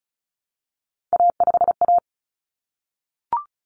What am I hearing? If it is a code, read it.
Morse code, heard as a keyed tone, about 34 wpm, 700 Hz. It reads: A5A
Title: Common Cut Numbers - 3-Digit 35wpm